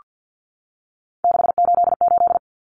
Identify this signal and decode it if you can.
Morse code, heard as a keyed tone, about 48 wpm, 700 Hz. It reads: 688
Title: Numbers - 3-Digit 50wpm